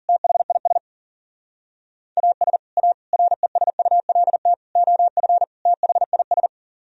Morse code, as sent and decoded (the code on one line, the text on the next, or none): - .... .. ... / .- ... / .- / .-. ... ..- .-.. - / --- ..-. / - .... .. ...
THIS AS A RESULT OF THIS